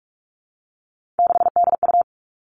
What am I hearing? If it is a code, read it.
Morse code, heard as a keyed tone, about 44 wpm, 700 Hz. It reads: 6DU
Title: All Cut Numbers - 3-Digit 45wpm